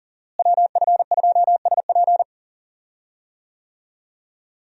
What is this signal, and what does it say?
Morse code, heard as a keyed tone, about 40 wpm, 700 Hz. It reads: WF2SP
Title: USA Call Signs - 5 Characters - Repeated 3x - 40wpm